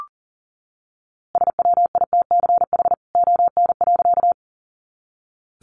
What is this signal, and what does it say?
Morse code, heard as a keyed tone, about 40 wpm, 700 Hz. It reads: SWITCH ON.
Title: A Sound Of Thunder 40wpm